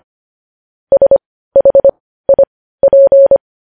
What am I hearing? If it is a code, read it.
Morse code, heard as a keyed tone, about 25 wpm, spaced as for 15 wpm, 550 Hz. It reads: SHIP
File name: podcast / News headlines-1x-25x15